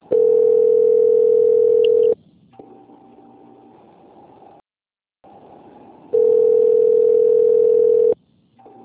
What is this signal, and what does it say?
Ringback tone (North America), 440 + 480 Hz.
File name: out- - - - - .75490